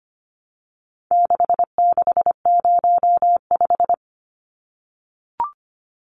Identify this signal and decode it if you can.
Morse code, heard as a keyed tone, about 25 wpm, 700 Hz. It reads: 6605
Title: Numbers - 4-Digit 25wpm